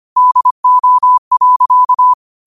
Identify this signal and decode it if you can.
Morse code, heard as a keyed tone, about 25 wpm, 1000 Hz. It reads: DO.
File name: qotdfiles / 25wpm